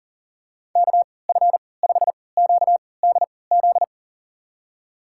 Morse code, heard as a keyed tone, about 40 wpm, spaced as for 23 wpm, 700 Hz. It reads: KF5QDZ
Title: USA Call Signs - 6 Characters - 3x Character Spacing - 40wpm